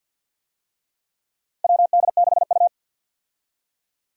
Morse code, heard as a keyed tone, about 47 wpm, 700 Hz. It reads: WD6U
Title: USA Call Signs - 4 Characters - 50wpm